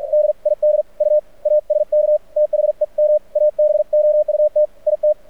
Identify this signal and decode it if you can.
Morse code, heard as a keyed tone, about 47 wpm, 600 Hz. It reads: DID U ASK THEM ABOUT IT
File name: archived / CRSnet